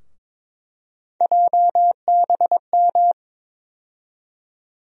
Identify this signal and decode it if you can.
Morse code, heard as a keyed tone, about 22 wpm, 700 Hz. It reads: JBM
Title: USA Call Sign Postfixes - 3 Letters 22wpm